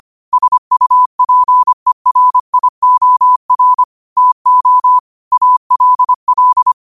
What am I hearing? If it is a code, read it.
Morse code, heard as a keyed tone, about 25 wpm, 1000 Hz. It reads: SUPERIOR TO ALL
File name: qotdfiles / 25wpm